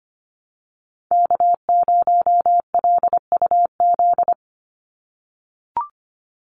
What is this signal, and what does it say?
Morse code, heard as a keyed tone, about 25 wpm, 700 Hz. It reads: K0LUZ
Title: Call Signs - 25wpm